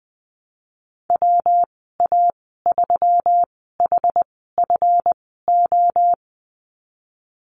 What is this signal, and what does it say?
Morse code, heard as a keyed tone, about 20 wpm, spaced as for 15 wpm, 700 Hz. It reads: WA3HFO
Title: 173 Lesson 28g - Number 9 - Call Signs 20wpm